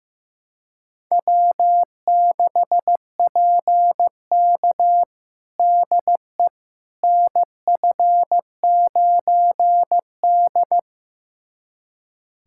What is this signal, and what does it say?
Morse code, heard as a keyed tone, about 15 wpm, 700 Hz. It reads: W6PK DE NF9D
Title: USA Callsign DE Callsign - 4 Characters - 15wpm